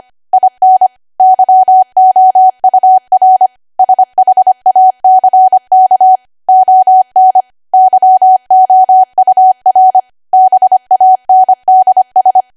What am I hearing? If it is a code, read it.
Morse code, heard as a keyed tone, about 25 wpm, 750 Hz. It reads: IN YOUR SHACK ON YOUR BANDS
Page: 25 WPM Code Archive